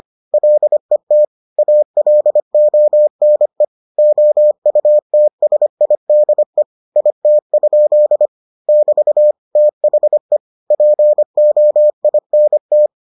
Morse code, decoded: LET ALONE OUTSIDE IT? <BT> THE POINT

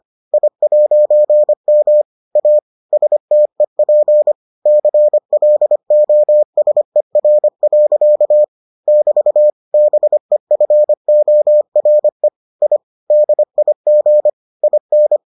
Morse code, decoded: I'M A STEP CLOSER. <BT> BEFORE I DIG IN

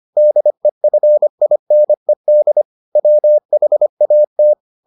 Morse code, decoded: DEFINED WHAT

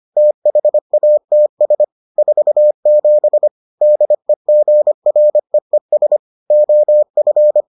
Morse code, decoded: THATS 47 DEGREES OF

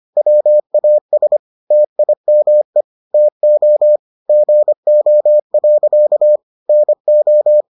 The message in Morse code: WAS TIME TO GO. NO

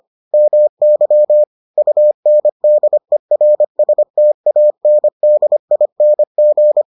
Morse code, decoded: MY UNDERSTANDING

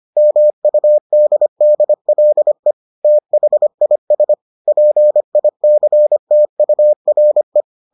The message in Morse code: MUDDLE THIS PICTURE

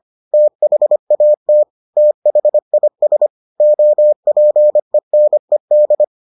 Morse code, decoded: THAT THIS OPENED

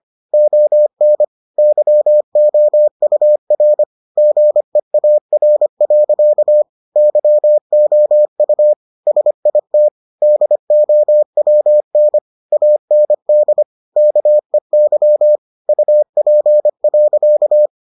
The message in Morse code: ON YOUR GEAR. YOU SIT DOWN AND KEY UP.

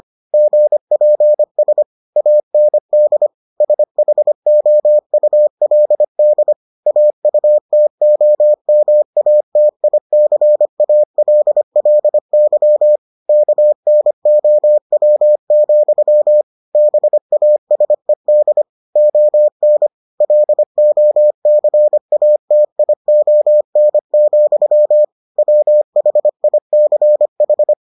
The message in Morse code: GPS AND SHOULD AUTOMATICALLY KNOW, BASED ON LOCATION, WHICH

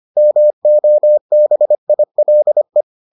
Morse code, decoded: MOBILE

